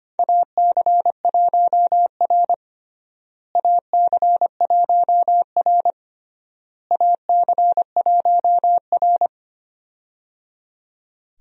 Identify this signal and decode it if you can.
Morse code, heard as a keyed tone, about 25 wpm, 700 Hz. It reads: AC1R AC1R AC1R
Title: USA Call Signs - 4 Characters - Repeated 3x - 25wpm